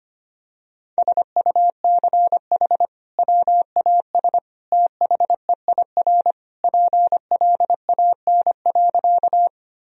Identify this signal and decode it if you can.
Morse code, heard as a keyed tone, about 25 wpm, 700 Hz. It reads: SUCH WAS THEIR PLAN.
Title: Sentences from Top 1000 Words - Review Part 2 - 25wpm